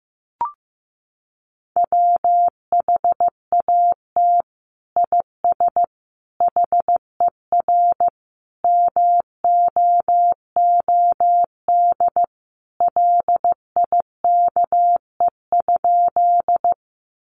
Morse code, heard as a keyed tone, about 15 wpm, 700 Hz. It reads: WHAT IS HER MOOD LIKE?
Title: Sentences from Top 1100 Words 15wpm